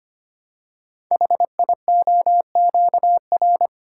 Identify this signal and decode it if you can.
Morse code, heard as a keyed tone, about 25 wpm, 700 Hz. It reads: HIOQR
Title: Code Groups - 5 Letter:Numbers - 25wpm